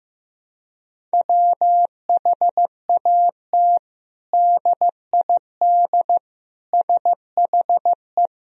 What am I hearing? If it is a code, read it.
Morse code, heard as a keyed tone, about 15 wpm, 700 Hz. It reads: WHAT DID SHE
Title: Sentences from Top 1000 Words - Review Part 4 - 15wpm